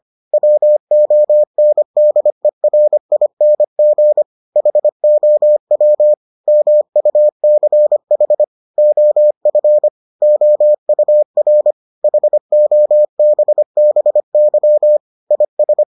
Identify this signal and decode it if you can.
Morse code, heard as a keyed tone, about 25 wpm, 600 Hz. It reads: WONDERING HOW MUCH OF OUR HOBBY IS